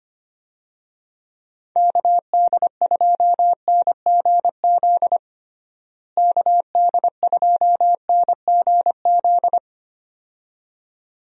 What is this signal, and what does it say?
Morse code, heard as a keyed tone, about 25 wpm, 700 Hz. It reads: KD2NGZ KD2NGZ
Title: USA Call Signs - 6 Characters - Repeated 2x - 25wpm